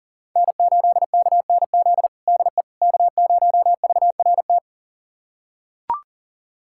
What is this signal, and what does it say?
Morse code, heard as a keyed tone, about 37 wpm, 700 Hz. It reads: N8KNZ DE K0VRT
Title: USA Callsign DE Callsign - 5 Characters - 40wpm